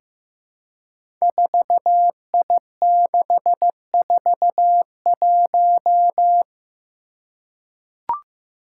Morse code, heard as a keyed tone, about 15 wpm, 700 Hz. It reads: 4I641
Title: Five Challenging Letter-Number Combinations 15wpm